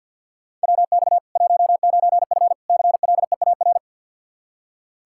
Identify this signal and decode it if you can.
Morse code, heard as a keyed tone, about 46 wpm, 700 Hz. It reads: WX 19F CLEAR
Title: Practice QSO - Exchange 2 - Info - 50wpm